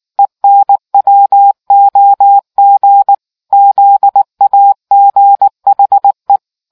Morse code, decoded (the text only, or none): ENWOG ZAGHE